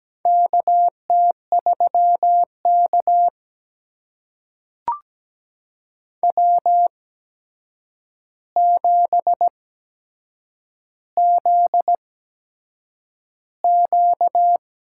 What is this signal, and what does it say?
Morse code, heard as a keyed tone, about 17 wpm, spaced as for 7 wpm, 700 Hz. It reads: KT3K W7ZQ
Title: USA Call Signs - 4 Characters - 8x Character Spacing - 17wpm